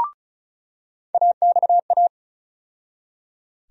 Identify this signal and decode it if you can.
Morse code, heard as a keyed tone, about 36 wpm, 700 Hz. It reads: AXA
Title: USA Call Sign Postfixes - 3 Letters 35wpm